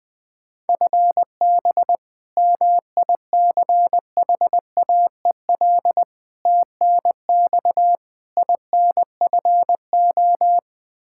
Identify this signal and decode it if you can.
Morse code, heard as a keyed tone, about 20 wpm, 700 Hz. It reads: FB MICHAEL TNX INFO
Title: Practice QSO - Exchange 3 - More Info - 20wpm